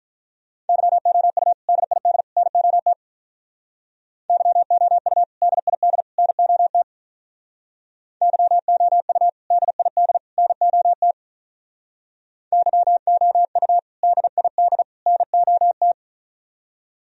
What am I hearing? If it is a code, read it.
Morse code, heard as a keyed tone, about 45 wpm, 700 Hz. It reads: YOU DID NOT YOU DID NOT YOU DID NOT YOU DID NOT